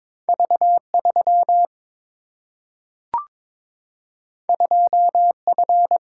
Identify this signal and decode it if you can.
Morse code, heard as a keyed tone, about 22 wpm, 700 Hz. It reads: V 3 2 F
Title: Single-Letter Single-Number 22wpm